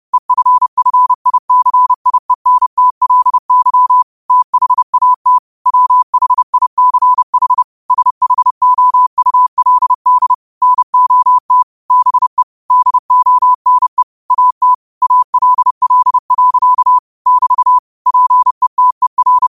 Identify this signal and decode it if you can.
Morse code, heard as a keyed tone, about 30 wpm, 1000 Hz. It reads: EFFICIENTLY THAT WHICH SHOULD NOT BE DONE AT ALL. <BT> PETER
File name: qotdfiles / 30wpm